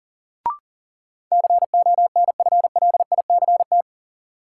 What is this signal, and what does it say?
Morse code, heard as a keyed tone, about 40 wpm, 700 Hz. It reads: CONFLICT
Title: Sentences from Top 1300 Words - New Word Drill - 40wpm